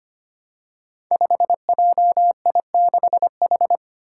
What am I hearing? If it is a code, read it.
Morse code, heard as a keyed tone, about 25 wpm, 700 Hz. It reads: 5JI6H
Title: Five Challenging Letter-Number Combinations 25wpm